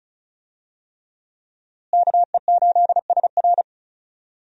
Morse code, heard as a keyed tone, about 36 wpm, 700 Hz. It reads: KE8SR
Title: USA Call Signs - 5 Characters - Repeated 3x - 35wpm